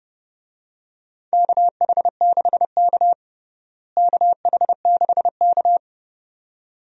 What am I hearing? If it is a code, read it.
Morse code, heard as a keyed tone, about 32 wpm, 700 Hz. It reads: KH6K KH6K
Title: DXCC-100 Call Signs - 4 Characters - Repeated 2x - 30wpm